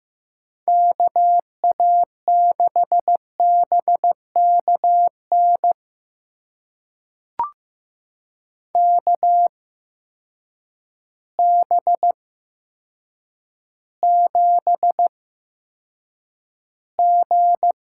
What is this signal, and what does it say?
Morse code, heard as a keyed tone, about 15 wpm, spaced as for 8 wpm, 700 Hz. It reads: KA6BKN KB7G
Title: USA Call Signs - 6 Characters - 8x Character Spacing - 15wpm